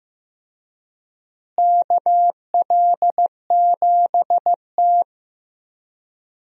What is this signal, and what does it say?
Morse code, heard as a keyed tone, about 15 wpm, 700 Hz. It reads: KL7T
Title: DXCC-100 Call Signs - 4 Characters - Repeated 2x - 15wpm